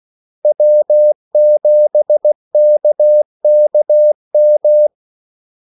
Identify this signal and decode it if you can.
Morse code, heard as a keyed tone, about 16 wpm, 600 Hz. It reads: W7KKM
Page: Kaldesignaler 16 wpm